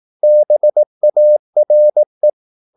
Morse code, heard as a keyed tone, about 18 wpm, 600 Hz. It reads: BARE
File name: Korte ord DK 18wpm